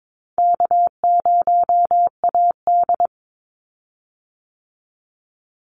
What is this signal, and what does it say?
Morse code, heard as a keyed tone, about 22 wpm, 700 Hz. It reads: K0AD
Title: Easy Call Signs Repeated 2x - 22wpm